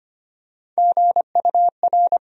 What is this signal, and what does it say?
Morse code, heard as a keyed tone, about 25 wpm, 700 Hz. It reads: GUR
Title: USA Call Sign Postfixes - 3 Letters 25wpm